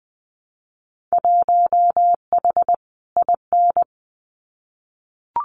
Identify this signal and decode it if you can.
Morse code, heard as a keyed tone, about 20 wpm, 700 Hz. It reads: 1H IN
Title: Winter Field Day Exchanges 20wpm